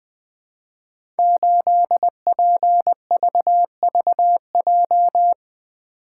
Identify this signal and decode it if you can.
Morse code, heard as a keyed tone, about 20 wpm, 700 Hz. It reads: 8PVVJ
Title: Code Groups - 5 Letter:Numbers - 20wpm